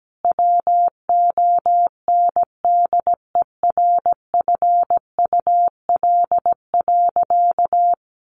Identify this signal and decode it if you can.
Morse code, heard as a keyed tone, about 17 wpm, 700 Hz. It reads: WONDERFUL.
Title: Sentences from Top 1400 Words 17wpm